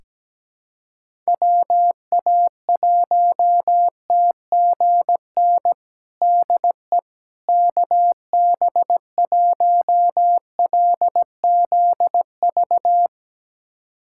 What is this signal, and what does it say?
Morse code, heard as a keyed tone, about 17 wpm, 700 Hz. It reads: WA1TGN DE KB1LZV
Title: USA Callsign DE Callsign - 6 Characters - 17wpm